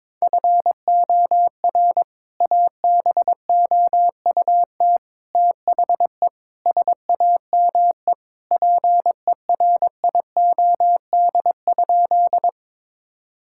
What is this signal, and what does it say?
Morse code, heard as a keyed tone, about 22 wpm, 700 Hz. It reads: FOR ABOUT THE SAME PERIOD?
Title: Sentences from Top 1000 Words - Review Part 5 - 22wpm